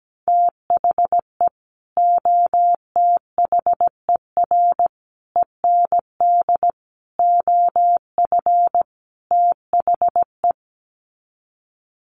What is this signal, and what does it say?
Morse code, heard as a keyed tone, about 17 wpm, 700 Hz. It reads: THE OTHER END OF THE
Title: Sets of 5 Words 17wpm